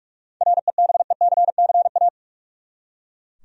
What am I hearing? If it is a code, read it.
Morse code, heard as a keyed tone, about 41 wpm, 700 Hz. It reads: REBECCA
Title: USA Common Names - 45wpm